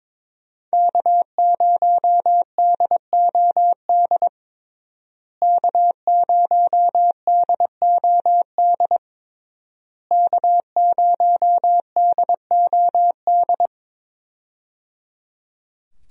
Morse code, heard as a keyed tone, about 22 wpm, 700 Hz. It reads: K0DOD K0DOD K0DOD
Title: USA Call Signs - 5 Characters - Repeated 3x - 22wpm